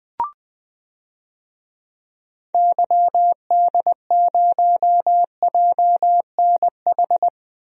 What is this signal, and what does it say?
Morse code, heard as a keyed tone, about 20 wpm, 700 Hz. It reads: YD0JNH
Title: DXCC-100 Call Signs - 6 Characters - 20wpm